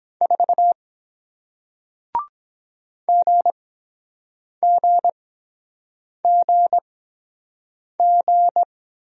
Morse code, heard as a keyed tone, about 23 wpm, spaced as for 7 wpm, 700 Hz. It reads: 4 GGGG